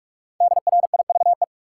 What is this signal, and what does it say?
Morse code, heard as a keyed tone, about 46 wpm, 700 Hz. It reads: DRIVE
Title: Five Letter Words - 45wpm